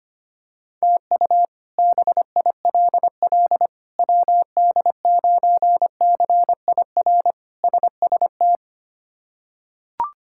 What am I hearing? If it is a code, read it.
Morse code, heard as a keyed tone, about 25 wpm, 700 Hz. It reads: TU BILL WD9CIR SST
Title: K1USN SST QSO Practice - 25wpm